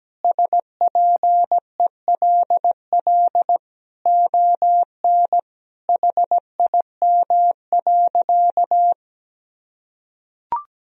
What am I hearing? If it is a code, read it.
Morse code, heard as a keyed tone, about 17 wpm, 700 Hz. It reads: SPELL ON HIM.
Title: Sentences from Top 1000 Words - Review Part 1 - 17wpm